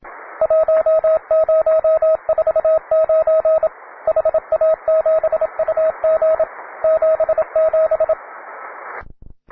Morse code, heard as a keyed tone, about 27 wpm, 640 Hz, with radio noise. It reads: 1049 HA7UG 77
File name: wae / cw beginner / ra9ac cw